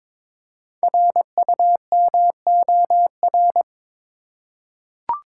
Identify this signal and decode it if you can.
Morse code, heard as a keyed tone, about 22 wpm, 700 Hz. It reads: RUMOR
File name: 5 Letter Words - First 3 Letters Given - 22wpm